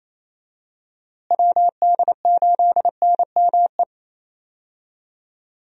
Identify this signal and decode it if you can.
Morse code, heard as a keyed tone, about 27 wpm, 700 Hz. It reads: WD8NME
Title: USA Call Signs - 6 Characters - Repeated 2x - 28wpm